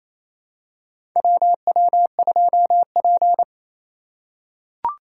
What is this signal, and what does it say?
Morse code, heard as a keyed tone, about 27 wpm, 700 Hz. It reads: WW2P
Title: K1USN SST QSO Practice - 28wpm